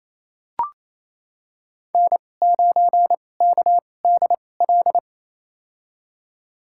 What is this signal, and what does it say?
Morse code, heard as a keyed tone, about 28 wpm, spaced as for 20 wpm, 700 Hz. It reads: N9KDL
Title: USA Call Signs - 2x Character Spacing - 28wpm